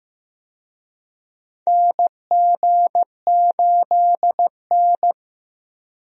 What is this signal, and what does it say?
Morse code, heard as a keyed tone, about 15 wpm, 700 Hz. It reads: NG8N
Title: USA Call Signs - 4 Characters - Repeated 2x - 15wpm